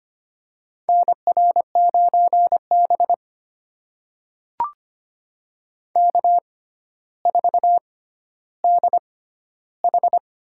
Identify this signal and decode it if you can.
Morse code, heard as a keyed tone, about 25 wpm, spaced as for 13 wpm, 700 Hz. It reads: NR9B K4DH
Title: USA Call Signs - 4 Characters - 6x Character Spacing - 25wpm